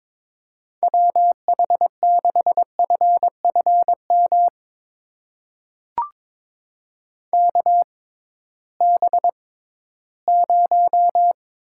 Morse code, heard as a keyed tone, about 22 wpm, spaced as for 14 wpm, 700 Hz. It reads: WH6FFM KB0
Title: USA Call Signs - 6 Characters - 6x Character Spacing - 22wpm